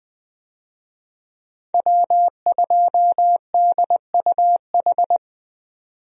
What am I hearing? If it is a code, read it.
Morse code, heard as a keyed tone, about 20 wpm, 700 Hz. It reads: W2DUH
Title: USA Call Signs - 5 Characters - 20wpm